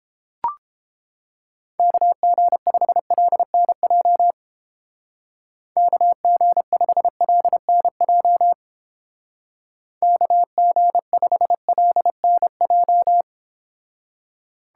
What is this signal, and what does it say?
Morse code, heard as a keyed tone, about 29 wpm, 700 Hz. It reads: KG5LNJ KG5LNJ KG5LNJ